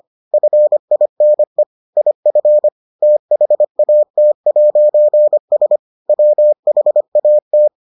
Morse code, heard as a keyed tone, about 25 wpm, 600 Hz. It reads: FINE IF THAT'S WHAT